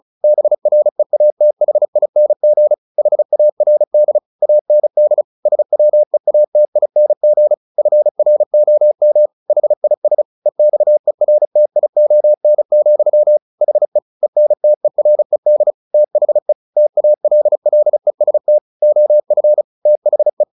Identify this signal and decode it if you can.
Morse code, heard as a keyed tone, about 34 wpm, 600 Hz. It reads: BREATHING HARD AND SWEATING FROM HIS EXERTION, HE ENTERED THE TALLEST OF THE